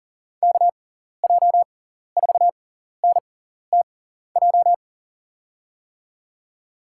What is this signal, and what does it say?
Morse code, heard as a keyed tone, about 40 wpm, spaced as for 14 wpm, 690 Hz. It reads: KJ4NTJ
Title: USA Call Signs - 6x Character Spacing - 40wpm